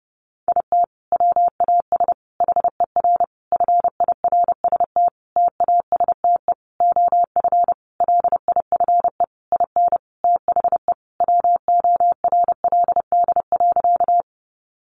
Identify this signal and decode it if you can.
Morse code, heard as a keyed tone, about 30 wpm, 700 Hz. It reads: IT WAS HER FIRST TASTE OF LIFE IN THE WORLD.
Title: Sentences from Top 1100 Words 30wpm